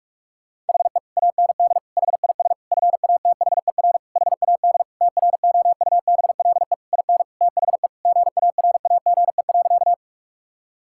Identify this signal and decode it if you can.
Morse code, heard as a keyed tone, about 42 wpm, 700 Hz. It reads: HE AND HIS FATHER HAD TROUBLE IN THE GARAGE.
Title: Sentences from Top 1600 Words 45wpm